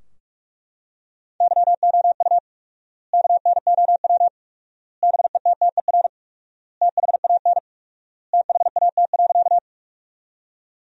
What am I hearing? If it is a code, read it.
Morse code, heard as a keyed tone, about 44 wpm, 700 Hz. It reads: YOU KNOW BETTER THAN THAT.
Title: 4x Word-Spacing - Sentences from Top 400 Words - 45wpm